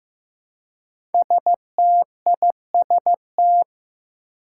Morse code, heard as a keyed tone, about 15 wpm, 700 Hz. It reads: STIST
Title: Five Challenging Letter-Number Combinations 15wpm